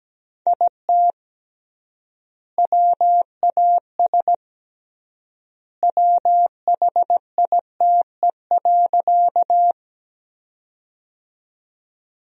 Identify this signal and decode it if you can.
Morse code, heard as a keyed tone, about 17 wpm, 700 Hz. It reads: IT WAS WHITE.
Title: 3x Word-Spacing - Sentences from Top 300 Words - 17wpm